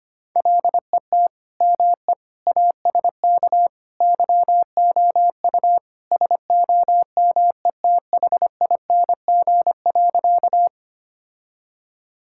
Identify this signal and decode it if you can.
Morse code, heard as a keyed tone, about 25 wpm, 700 Hz. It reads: LET ME ASK YOU SOMETHING.
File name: practice / 25wpm / Sentences with 4-Word Phrases - 25wpm